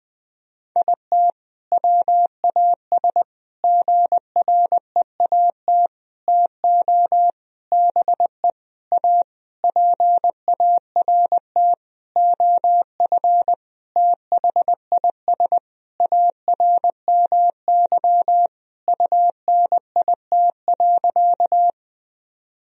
Morse code, heard as a keyed tone, about 20 wpm, 700 Hz. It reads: IT WAS GREAT TO BE A PART OF THIS ARMY UNIT.
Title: Sentences from Top 1000 Words - Review Part 2 - 20wpm